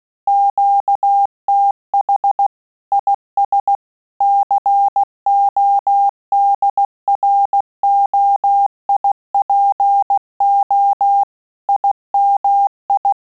Morse code, heard as a keyed tone, about 16 wpm, 800 Hz. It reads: QTH IS CODROIPO IMI
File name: QSO CW 75 CPM